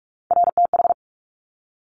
Morse code, heard as a keyed tone, about 46 wpm, 690 Hz. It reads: RTH
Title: Top Three Letter Combinations in Words 45wpm